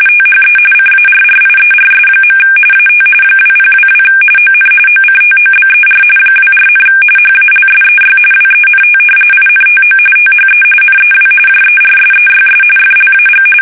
RTTY, about 45.45 baud. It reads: THE CARRIERS. AT 2022 ON OCTOBER 24, HALSEY WENT INTO THE NEW JERSEYS FLAG
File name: fsk850 / battleshipnj